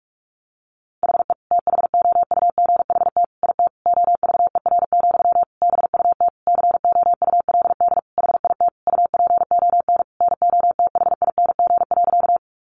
Morse code, heard as a keyed tone, about 44 wpm, 700 Hz. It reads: HE THOUGHT IT OVER, BUT COULD HIT UPON NOTHING.
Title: Sentences from Top 1200 Words 45wpm